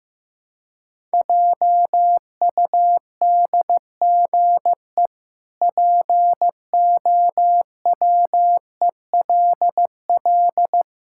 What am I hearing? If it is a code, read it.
Morse code, heard as a keyed tone, about 16 wpm, 700 Hz. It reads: JUDGE POWELL
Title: Pangram Challenge v2 - 15wpm